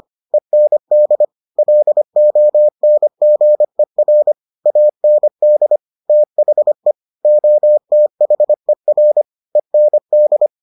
Morse code, heard as a keyed tone, about 25 wpm, 600 Hz. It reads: END LONGER AND THE OTHER END